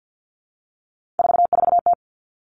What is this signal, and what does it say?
Morse code, heard as a keyed tone, about 50 wpm, 700 Hz. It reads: 44T
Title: Three Challenging Letter-Number Combinations 50wpm